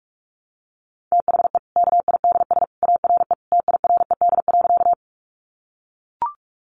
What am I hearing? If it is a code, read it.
Morse code, heard as a keyed tone, about 44 wpm, 700 Hz. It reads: THE KIDS ARE TIRED.
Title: Easy Phrases 45wpm